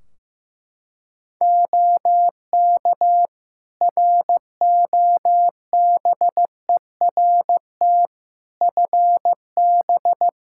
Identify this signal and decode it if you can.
Morse code, heard as a keyed tone, about 15 wpm, 700 Hz. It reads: OK ROBERT FB
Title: K1USN SST QSO Practice 15wpm